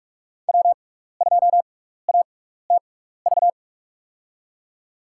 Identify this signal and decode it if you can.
Morse code, heard as a keyed tone, about 44 wpm, spaced as for 15 wpm, 700 Hz. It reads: W2ATV
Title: USA Call Signs - 5 Characters - 6x Character Spacing - 45wpm